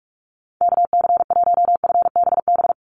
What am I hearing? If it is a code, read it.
Morse code, heard as a keyed tone, about 46 wpm, 700 Hz. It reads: KC1FBB
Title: USA Call Signs 45wpm